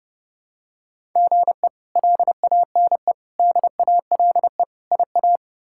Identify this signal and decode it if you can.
Morse code, heard as a keyed tone, about 32 wpm, 700 Hz. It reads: GE LANE DALE IA
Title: K1USN SST QSO Practice - 30wpm